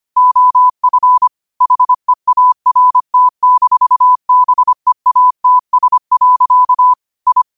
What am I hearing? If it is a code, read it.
Morse code, heard as a keyed tone, about 25 wpm, 1000 Hz. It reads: OF HEART-BEATS. I
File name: qotdfiles / 25wpm